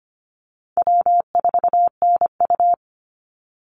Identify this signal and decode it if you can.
Morse code, heard as a keyed tone, about 25 wpm, 700 Hz. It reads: W4NU
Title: Easy Call Signs Repeated 3x - 25wpm